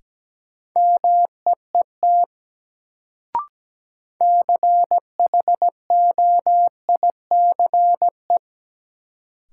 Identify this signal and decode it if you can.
Morse code, heard as a keyed tone, about 17 wpm, 700 Hz. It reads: MEET CHOICE
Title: Sets of 5 Words - New Word Drill - 17wpm